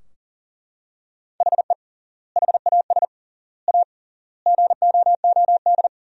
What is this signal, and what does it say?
Morse code, heard as a keyed tone, about 40 wpm, 700 Hz. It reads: HE HAS A GOOD
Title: 3x Word-Spacing - Sentences from Top 500 Words - 40wpm